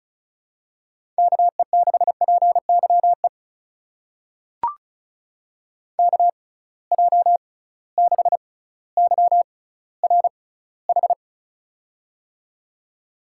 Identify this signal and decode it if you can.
Morse code, heard as a keyed tone, about 36 wpm, spaced as for 18 wpm, 700 Hz. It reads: KE6PYE KJ6YRH
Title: USA Call Signs - 6 Characters - 6x Character Spacing - 35wpm